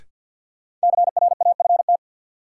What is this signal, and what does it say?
Morse code, heard as a keyed tone, about 53 wpm, 700 Hz. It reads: CRAFT
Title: Sentences from Top 1000 Words - Review Part 6 - New Word Drill - 50wpm